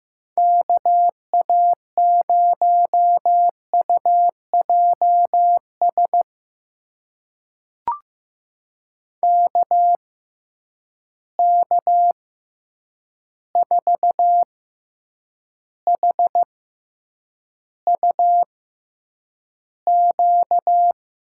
Morse code, heard as a keyed tone, about 15 wpm, spaced as for 8 wpm, 700 Hz. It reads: KA0UJS KK4HUQ
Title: USA Call Signs - 6 Characters - 6x Character Spacing - 15wpm